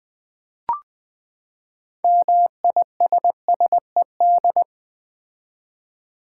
Morse code, heard as a keyed tone, about 20 wpm, 700 Hz. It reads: MISSED
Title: Common QSO Phrases - 4x Word-Spacing - 20wpm